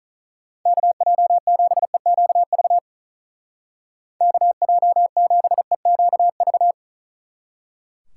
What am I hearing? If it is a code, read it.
Morse code, heard as a keyed tone, about 43 wpm, 700 Hz. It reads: KJ7EQV KJ7EQV
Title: Speed Racing - USA 2x3 Call Signs 35wpm